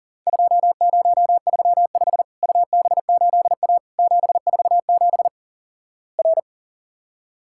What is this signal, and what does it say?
Morse code, heard as a keyed tone, about 43 wpm, 700 Hz. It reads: 2035 UB8A 747 R
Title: Worked All Europe DX Contest - QTC Exchange - 40wpm